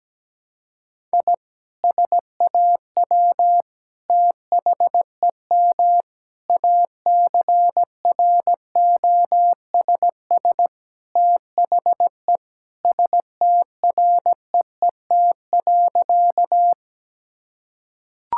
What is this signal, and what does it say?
Morse code, heard as a keyed tone, about 17 wpm, 700 Hz. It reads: I SAW THEM ACROSS THE STREET.
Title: Sentences from Top 1000 Words - Review Part 5 - 17wpm